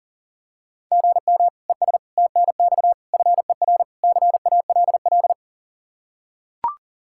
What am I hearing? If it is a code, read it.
Morse code, heard as a keyed tone, about 40 wpm, 700 Hz. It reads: GM ES TNX FER CALL
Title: Practice QSO - Exchange 1 - Beginning - 40wpm